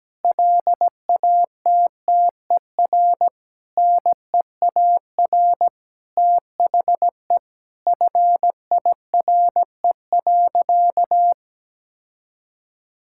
Morse code, heard as a keyed tone, about 17 wpm, 700 Hz. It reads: LATTER NEAR THE FIRE.
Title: Sentences from Top 1600 Words 17wpm